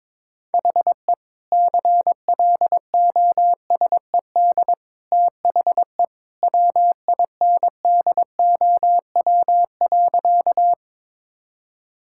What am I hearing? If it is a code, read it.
Morse code, heard as a keyed tone, about 22 wpm, 700 Hz. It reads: HE CLOSED THE WINDOW.
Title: Sentences from Top 1600 Words 22wpm